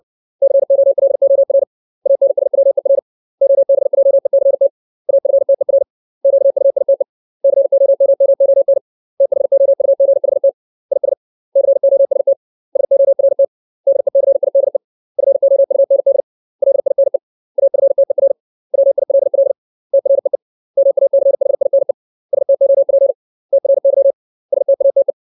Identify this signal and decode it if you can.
Morse code, heard as a keyed tone, about 60 wpm, 550 Hz. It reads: COLOR ANSWER OBJECT AFTER QUITE COMMON THOUGHT IS COST SORT DECIDE FOUND LINE AFTER WILL TREE MACHINE STOP TRY STATE
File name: 632 words / 632 Words 60WPM